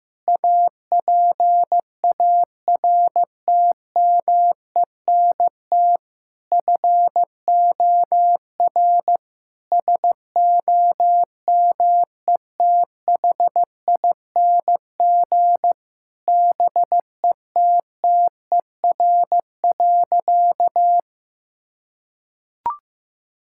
Morse code, heard as a keyed tone, about 16 wpm, 700 Hz. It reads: APARTMENT FOR SOMETHING BETTER.
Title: Sentences from Top 1000 Words - Review Part 6 - 15wpm